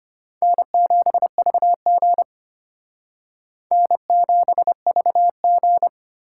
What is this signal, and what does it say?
Morse code, heard as a keyed tone, about 32 wpm, 700 Hz. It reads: N7VG N7VG